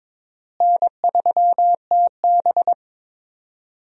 Morse code, heard as a keyed tone, about 22 wpm, 700 Hz. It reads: N3TB
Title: USA Call Signs - 4 Characters - Repeated 3x - 22wpm